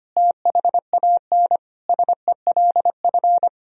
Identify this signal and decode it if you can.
Morse code, heard as a keyed tone, about 25 wpm, 700 Hz. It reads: THAN SELF
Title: Sentences from Top 1000 Words - Review Part 1 - 25wpm